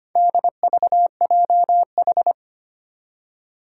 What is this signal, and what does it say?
Morse code, heard as a keyed tone, about 25 wpm, 700 Hz. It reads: DVJH
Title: Four Challenging Letter-Number Combinations 25wpm